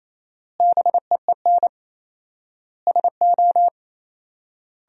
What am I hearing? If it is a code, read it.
Morse code, heard as a keyed tone, about 28 wpm, 700 Hz. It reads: BEEN SO
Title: 4x Word-Spacing - Sentences from Top 100 Words - 28wpm